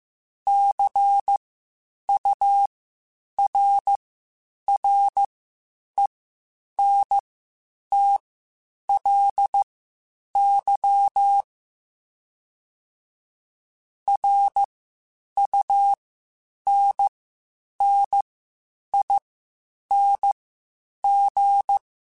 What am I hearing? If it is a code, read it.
Morse code, heard as a keyed tone, about 15 wpm, spaced as for 8 wpm, 800 Hz. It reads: CURRENTLY RUNNING
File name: (spaced) / QSOs Spaced-mp3s / WD0FBT 1 @13 sp